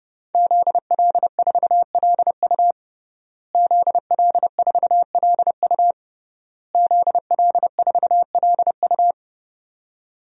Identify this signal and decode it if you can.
Morse code, heard as a keyed tone, about 32 wpm, 700 Hz. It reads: ZL4LU ZL4LU ZL4LU
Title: DXCC-100 Call Signs - 5 Characters - Repeated 3x - 30wpm